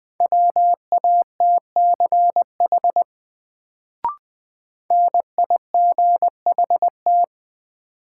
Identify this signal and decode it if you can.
Morse code, heard as a keyed tone, about 20 wpm, 700 Hz. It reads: WATCH NIGHT
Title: Sentences from Top 300 Words - New Word Drill - 20wpm